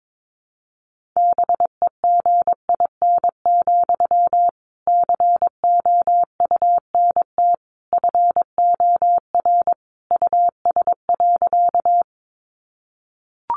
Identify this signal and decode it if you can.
Morse code, heard as a keyed tone, about 22 wpm, 700 Hz. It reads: BEGIN, COUNT FOR US.
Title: Sentences from Top 1500 Words 22wpm